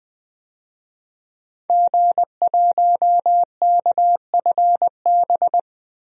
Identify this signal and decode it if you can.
Morse code, heard as a keyed tone, about 21 wpm, 700 Hz. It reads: G1KFB
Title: DXCC-100 Call Signs - 5 Characters - 20wpm